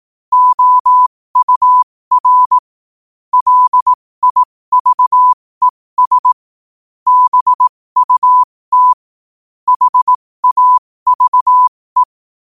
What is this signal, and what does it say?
Morse code, heard as a keyed tone, about 18 wpm, spaced as for 15 wpm, 1000 Hz. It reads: OUR LIVES BUT HAVE
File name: qotdfiles / 15wpm